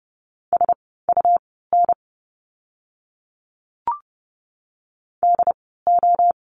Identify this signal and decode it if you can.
Morse code, heard as a keyed tone, about 30 wpm, spaced as for 17 wpm, 700 Hz. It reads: SUN DO
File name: Lesson 17f - Letter u - Words 30wpm